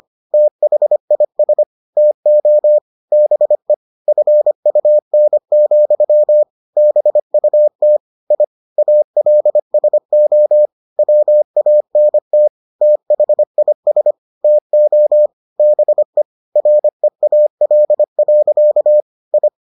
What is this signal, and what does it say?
Morse code, heard as a keyed tone, about 25 wpm, 600 Hz. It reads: THIS TO BE FUN, BUT I ALSO WANT THIS TO BE REAL. I